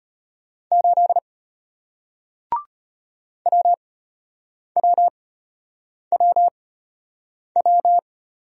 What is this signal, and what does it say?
Morse code, heard as a keyed tone, about 33 wpm, spaced as for 8 wpm, 700 Hz. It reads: 8 WWWW